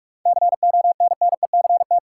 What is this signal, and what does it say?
Morse code, heard as a keyed tone, about 44 wpm, 700 Hz. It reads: CONNECT
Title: Sentences from Top 1500 Words - New Word Drill - 45wpm